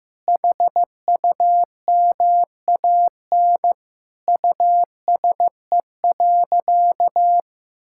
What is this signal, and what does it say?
Morse code, heard as a keyed tone, about 15 wpm, 700 Hz. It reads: HUMAN USE.
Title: Sentences from Top 1000 Words - Review Part 6 - 15wpm